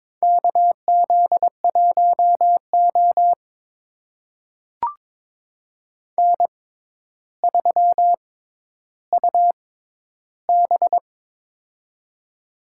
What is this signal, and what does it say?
Morse code, heard as a keyed tone, about 22 wpm, spaced as for 11 wpm, 700 Hz. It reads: KZ1O N3UB
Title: USA Call Signs - 4 Characters - 6x Character Spacing - 22wpm